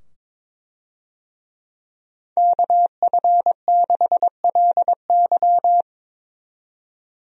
Morse code, heard as a keyed tone, about 22 wpm, 700 Hz. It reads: KF6LY
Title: USA Call Signs - 5 Characters - 22wpm